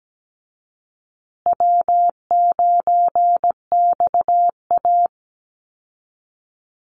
Morse code, heard as a keyed tone, about 17 wpm, 700 Hz. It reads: W9XA
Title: Call Signs Repeated 2x - 17wpm